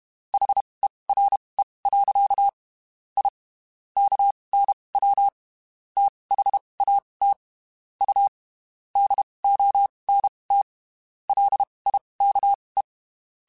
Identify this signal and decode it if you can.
Morse code, heard as a keyed tone, about 32 wpm, spaced as for 23 wpm, 800 Hz. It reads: HERE. I KNW THAT U DONT LIKE
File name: (spaced) / QSOs Spaced-mp3s / WD0FBT @30 sp